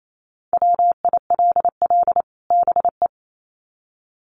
Morse code, heard as a keyed tone, about 28 wpm, 700 Hz. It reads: WILL BE
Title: Sets of 2 Words 28wpm